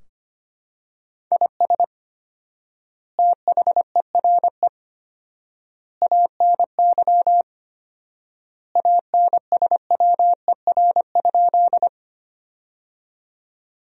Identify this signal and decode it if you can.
Morse code, heard as a keyed tone, about 25 wpm, 700 Hz. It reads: IS THERE ANY ANSWER?
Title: 4x Word-Spacing - Sentences from Top 300 Words - 25wpm